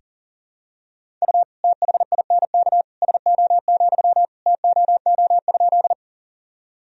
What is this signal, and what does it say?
Morse code, heard as a keyed tone, about 40 wpm, 700 Hz. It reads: U THINK SO, TOO?
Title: Sentences from Top 200 Words and Abbreviations 40wpm